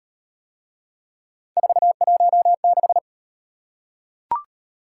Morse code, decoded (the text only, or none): 416